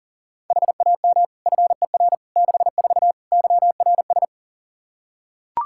Morse code, heard as a keyed tone, about 40 wpm, 700 Hz. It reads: HAM FER 64 YRS
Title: K1USN SST QSO Practice 40wpm